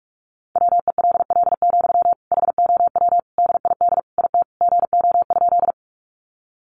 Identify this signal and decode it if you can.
Morse code, heard as a keyed tone, about 44 wpm, 700 Hz. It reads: WELL, HOW DID IT GO?
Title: Sentences from Top 200 Words 45wpm